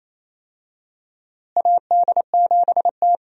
Morse code, heard as a keyed tone, about 28 wpm, 700 Hz. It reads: AD7T
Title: USA Call Signs - 4 Characters - Repeated 2x - 28wpm